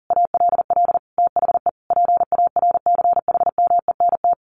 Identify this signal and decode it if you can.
Morse code, heard as a keyed tone, about 40 wpm, 700 Hz. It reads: ALL THE PARCHMENT
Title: A Sound Of Thunder 40wpm